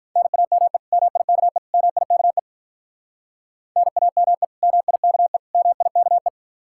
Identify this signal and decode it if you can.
Morse code, heard as a keyed tone, about 45 wpm, 700 Hz. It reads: NAME MIKE MIKE NAME MIKE MIKE